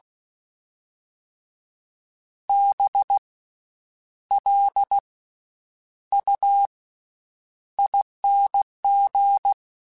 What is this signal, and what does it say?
Morse code, heard as a keyed tone, about 16 wpm, 800 Hz. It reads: B L U ING
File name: Special INGs-mp3s / Special INGs-mp3s / Slow ing @15